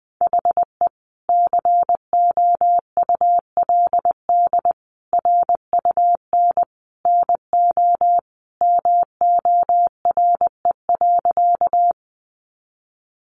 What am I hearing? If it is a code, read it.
Morse code, heard as a keyed tone, about 20 wpm, 700 Hz. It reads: HE COULD RUN NO MORE.
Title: Sentences from Top 300 Words 20wpm